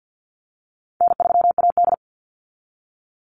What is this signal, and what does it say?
Morse code, heard as a keyed tone, about 52 wpm, 700 Hz. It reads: N3AD
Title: Easy Call Signs Repeated 3x - 50wpm